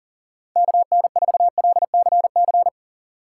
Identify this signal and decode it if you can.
Morse code, heard as a keyed tone, about 40 wpm, 700 Hz. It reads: KN4LCC
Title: USA Call Signs - 6 Characters - 3x Character Spacing - 40wpm